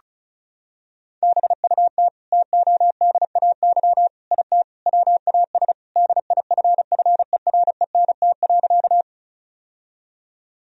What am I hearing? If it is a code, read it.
Morse code, heard as a keyed tone, about 36 wpm, 700 Hz. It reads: BUT TODAY IT WAS DIFFERENT.
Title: Sentences from Top 1000 Words - Review Part 4 - 35wpm